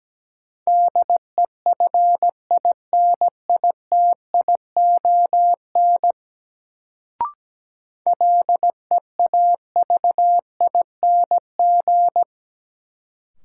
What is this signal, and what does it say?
Morse code, heard as a keyed tone, about 17 wpm, 700 Hz. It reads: DEFINITION LEAVING
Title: Sentences from Top 1000 Words - Review Part 4 - New Word Drill - 17wpm